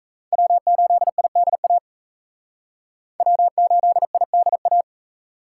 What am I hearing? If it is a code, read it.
Morse code, heard as a keyed tone, about 35 wpm, 700 Hz. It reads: W8IDA W8IDA
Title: Speed Racing - USA 1x3 Call Signs 28wpm